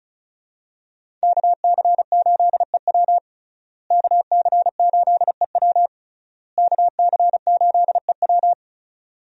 Morse code, heard as a keyed tone, about 36 wpm, 700 Hz. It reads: KC8EW KC8EW KC8EW
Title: USA Call Signs - 5 Characters - Repeated 3x - 35wpm